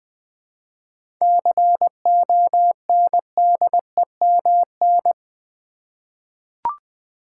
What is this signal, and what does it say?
Morse code, heard as a keyed tone, about 20 wpm, 700 Hz. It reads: CONDEMN
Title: 7 Letter Words - First 3 Letters Given - 20wpm